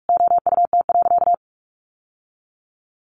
Morse code, heard as a keyed tone, about 44 wpm, 700 Hz. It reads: OUT.
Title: Sentences from Top 1500 Words 45wpm